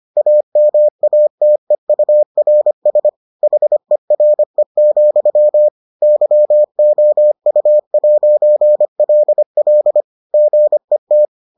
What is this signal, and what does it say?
Morse code, heard as a keyed tone, about 25 wpm, 600 Hz. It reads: AMATEURS HERE, YOU'LL GET